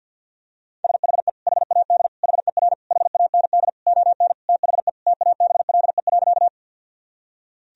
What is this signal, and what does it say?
Morse code, heard as a keyed tone, about 50 wpm, 700 Hz. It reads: SHE HAD HER HAND ON THE TABLE.
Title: Sentences from Top 1000 Words - Review Part 2 - 50wpm